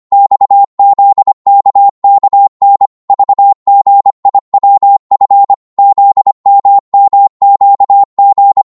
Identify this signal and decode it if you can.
Morse code, heard as a keyed tone, about 25 wpm, 800 Hz. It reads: XZKKN VGIWF ZMMQG
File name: Morse / ltr-25wpm